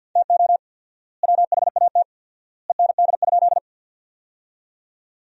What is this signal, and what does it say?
Morse code, heard as a keyed tone, about 50 wpm, 700 Hz. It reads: TO WHAT END?
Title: 4x Word-Spacing - Sentences from Top 200 Words - 50wpm